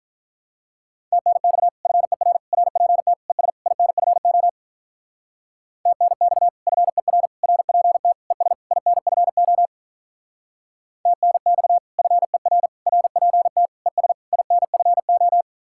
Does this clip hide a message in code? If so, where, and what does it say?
Yes: Morse code from 1.1 to 15.4 s, heard as a keyed tone, about 45 wpm, 700 Hz. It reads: TNX FER RPT ES INFO TNX FER RPT ES INFO TNX FER RPT ES INFO